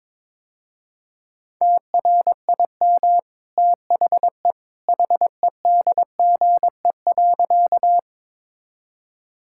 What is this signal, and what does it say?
Morse code, heard as a keyed tone, about 22 wpm, 700 Hz. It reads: TRIM THE HEDGE.
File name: practice / 22wpm / Easy Sentences with Hints - 22wpm